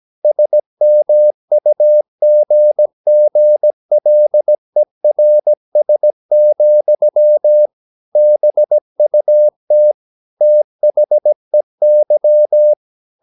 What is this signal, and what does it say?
Morse code, heard as a keyed tone, about 17 wpm, 600 Hz. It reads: SMUGGLERS, BUT THEY
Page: Romanen Never af Ken Follet på engelsk.